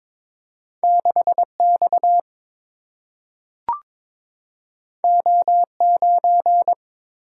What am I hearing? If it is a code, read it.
Morse code, heard as a keyed tone, about 22 wpm, 700 Hz. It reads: 6 X O 9
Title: Single-Letter Single-Number 22wpm